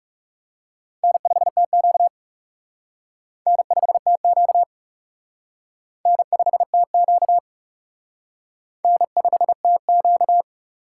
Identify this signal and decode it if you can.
Morse code, heard as a keyed tone, about 36 wpm, 700 Hz. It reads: N5TQ N5TQ N5TQ N5TQ
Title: Speed Racing - USA 1x2 Call Signs 30wpm